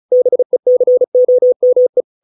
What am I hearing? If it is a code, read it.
Morse code, heard as a keyed tone, about 35 wpm, 500 Hz. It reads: BECOME